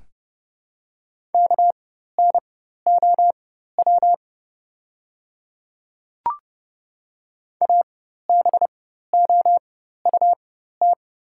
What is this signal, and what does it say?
Morse code, heard as a keyed tone, about 30 wpm, spaced as for 14 wpm, 700 Hz. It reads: KNOW ABOUT
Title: 4x Character Spacing - Top 200 Words - 30wpm